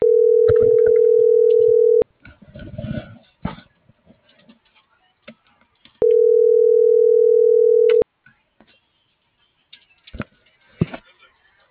Ringback tone (North America), 440 + 480 Hz.